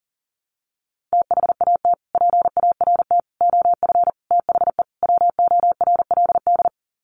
Morse code, heard as a keyed tone, about 40 wpm, 700 Hz. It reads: THAT PART OF THE WORLD
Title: Sets of 5 Words Encore2 40wpm